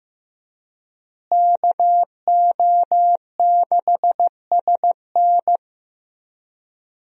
Morse code, heard as a keyed tone, about 15 wpm, 700 Hz. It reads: KO6SN
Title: USA Call Signs - 5 Characters - Repeated 3x - 15wpm